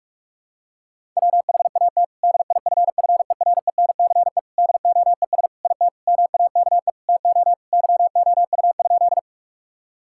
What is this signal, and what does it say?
Morse code, heard as a keyed tone, about 46 wpm, 700 Hz. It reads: WHAT DIFFERENCE DOES IT MAKE TO YOU?
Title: Sentences from Top 1000 Words - Review Part 5 - 45wpm